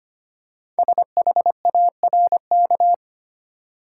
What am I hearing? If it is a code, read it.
Morse code, heard as a keyed tone, about 25 wpm, 700 Hz. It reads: SHARK
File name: practice / 25wpm / Easy Words with Hints - 25wpm